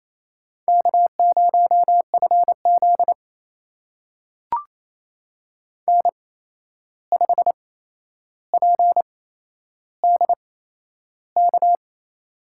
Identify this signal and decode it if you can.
Morse code, heard as a keyed tone, about 28 wpm, spaced as for 11 wpm, 700 Hz. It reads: K0FZ N5PDK
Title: USA Call Signs - 8x Character Spacing - 28wpm